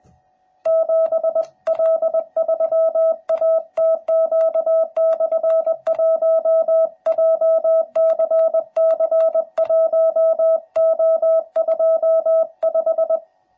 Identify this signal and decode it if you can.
Morse code, heard as a keyed tone, about 20 wpm, 650 Hz. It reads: 7L3ATQ/1 JCC1O25
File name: jh8cbh / 2021cw / cw12b